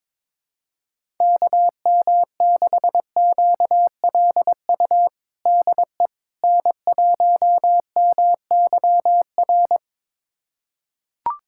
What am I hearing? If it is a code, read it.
Morse code, heard as a keyed tone, about 21 wpm, 700 Hz. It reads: KM6QLU DE N1MYR
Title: USA Callsign DE Callsign 22wpm